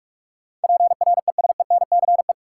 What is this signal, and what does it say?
Morse code, heard as a keyed tone, about 46 wpm, 700 Hz. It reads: PRESENCE
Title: Sets of 3 Words - New Word Drill - 45wpm